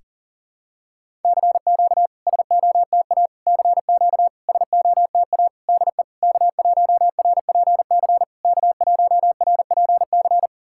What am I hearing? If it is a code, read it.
Morse code, heard as a keyed tone, about 38 wpm, 690 Hz. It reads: CQ SOTA CQ SOTA DE K1RPC K1RPC
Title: SOTA QSO Practice 40wpm